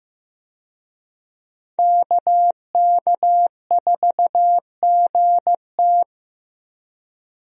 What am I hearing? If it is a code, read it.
Morse code, heard as a keyed tone, about 15 wpm, 700 Hz. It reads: KK4GT
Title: USA Call Signs - 5 Characters - Repeated 2x - 15wpm